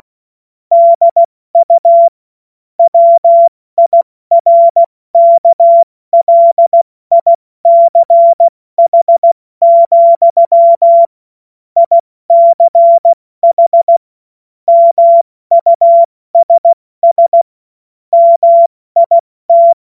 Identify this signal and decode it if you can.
Morse code, heard as a keyed tone, about 16 wpm, 680 Hz. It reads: DU WIRKLICH, ICH MUSS MIT
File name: Morsekurs / Witze / Witz 16 April 2024 70 F80